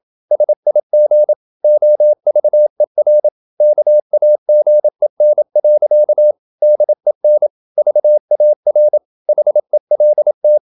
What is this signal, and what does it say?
Morse code, heard as a keyed tone, about 27 wpm, 600 Hz. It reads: SIG OVER KAGEN. DEN VAR HELT
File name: Ferie QRQ 27wpm